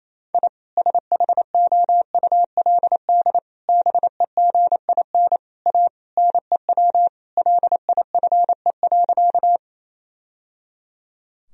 Morse code, heard as a keyed tone, about 28 wpm, 700 Hz. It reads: I SHOULD BEGIN A NEW LIFE.
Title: Sentences from Top 1000 Words - Review Part 1 - 28wpm